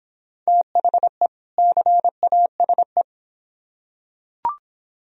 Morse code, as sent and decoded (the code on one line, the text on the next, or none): - .... / -.-. .- ... .
THE CASE